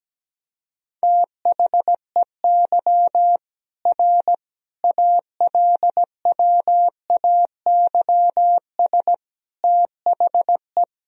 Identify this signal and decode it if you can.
Morse code, heard as a keyed tone, about 17 wpm, 700 Hz. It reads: THEY R ALWAYS THE
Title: Sentences from Top 300 Words and Abbreviations 17wpm